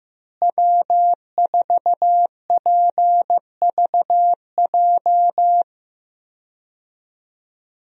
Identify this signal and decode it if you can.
Morse code, heard as a keyed tone, about 15 wpm, 700 Hz. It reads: W4PVJ
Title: USA Call Signs - 5 Characters - Repeated 2x - 15wpm